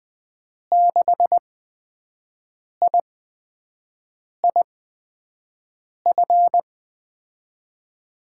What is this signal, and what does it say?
Morse code, heard as a keyed tone, about 20 wpm, spaced as for 5 wpm, 700 Hz. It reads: 6IIF
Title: USA Call Signs - 6 Characters - 8x Character Spacing - 20wpm